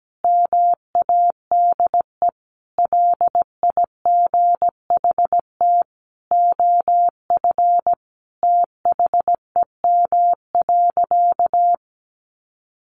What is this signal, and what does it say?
Morse code, heard as a keyed tone, about 17 wpm, 700 Hz. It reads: MADE LIGHT OF THEM.
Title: Sentences from Top 1300 Words 17wpm